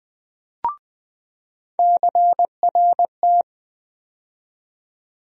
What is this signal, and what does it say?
Morse code, heard as a keyed tone, about 20 wpm, 700 Hz. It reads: CRT
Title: USA Call Sign Postfixes - 3 Letters 20wpm